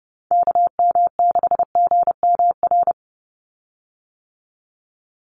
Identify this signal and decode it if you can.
Morse code, heard as a keyed tone, about 30 wpm, 700 Hz. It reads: KM6GMR
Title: USA Call Signs 30wpm